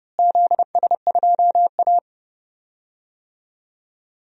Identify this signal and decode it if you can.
Morse code, heard as a keyed tone, about 31 wpm, 700 Hz. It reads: ZS2A
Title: DXCC-100 Call Signs - 4 Characters - Repeated 2x - 30wpm